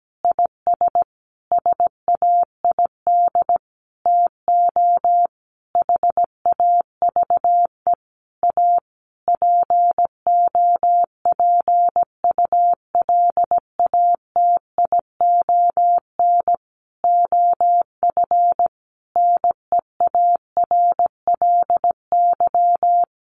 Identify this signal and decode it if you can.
Morse code, heard as a keyed tone, about 17 wpm, 700 Hz. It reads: IS SAID TO HAVE A POPULATION OF NEARLY
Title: Sentences from Top 1300 Words 17wpm